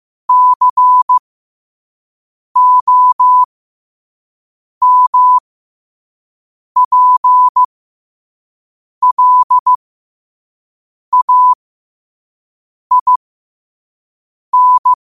Morse code, heard as a keyed tone, about 15 wpm, spaced as for 5 wpm, 1000 Hz. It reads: COMPLAIN
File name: qotdfiles / 5wpm